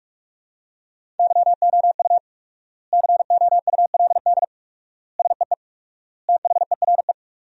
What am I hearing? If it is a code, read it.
Morse code, heard as a keyed tone, about 41 wpm, 700 Hz. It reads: YOU COULD SEE THERE
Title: 4x Word-Spacing - Sentences from Top 500 Words - 45wpm